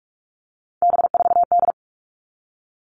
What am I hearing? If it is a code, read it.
Morse code, heard as a keyed tone, about 44 wpm, 700 Hz. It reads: B4D
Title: All Cut Numbers - 3-Digit 45wpm